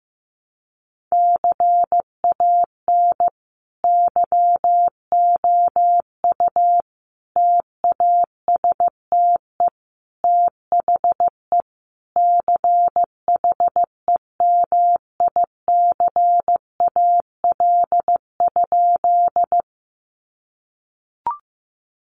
Morse code, heard as a keyed tone, about 15 wpm, 700 Hz. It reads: CAN YOU TASTE THE CHEMICAL?
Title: Sentences from Top 1200 Words 15wpm